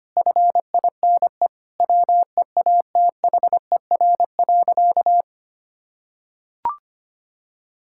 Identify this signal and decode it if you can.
Morse code, heard as a keyed tone, about 25 wpm, 700 Hz. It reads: FINE WEATHER.
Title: Sentences from Top 1000 Words - Review Part 5 - 25wpm